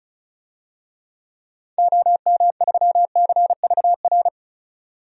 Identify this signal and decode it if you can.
Morse code, heard as a keyed tone, about 36 wpm, 700 Hz. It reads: OM3CVR
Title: DXCC-100 Call Signs - 6 Characters - 35wpm